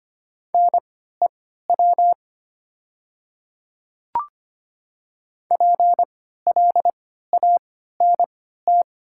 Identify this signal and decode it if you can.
Morse code, heard as a keyed tone, about 25 wpm, spaced as for 14 wpm, 700 Hz. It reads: NEW PLANT
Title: 3x Character Spacing - Top 300 Words - 25wpm